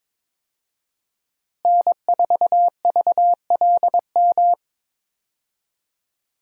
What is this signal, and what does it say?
Morse code, heard as a keyed tone, about 22 wpm, 700 Hz. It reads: N4VLM
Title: USA Call Signs - 5 Characters - Repeated 2x - 22wpm